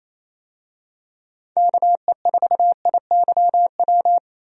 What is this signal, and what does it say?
Morse code, heard as a keyed tone, about 28 wpm, 700 Hz. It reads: KE4IYW
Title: USA Call Signs - 6 Characters - 28wpm